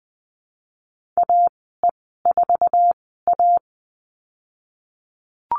Morse code, heard as a keyed tone, about 20 wpm, spaced as for 15 wpm, 700 Hz. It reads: AE4A
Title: 048 Lesson 9g - Number 4 - Call Signs 20wpm